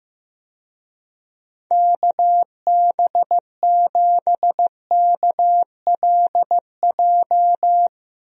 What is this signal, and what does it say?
Morse code, heard as a keyed tone, about 15 wpm, 700 Hz. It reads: KB7KLJ
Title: USA Call Signs - 6 Characters - Repeated 2x - 15wpm